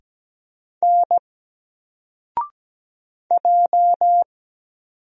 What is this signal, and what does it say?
Morse code, heard as a keyed tone, about 17 wpm, 700 Hz. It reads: N J
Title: Single Letter-Number Alt 17wpm